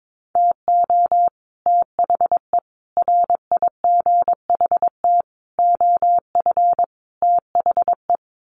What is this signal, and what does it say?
Morse code, heard as a keyed tone, about 22 wpm, 700 Hz. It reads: TO THE RIGHT OF THE
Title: Sets of 5 Words Encore 22wpm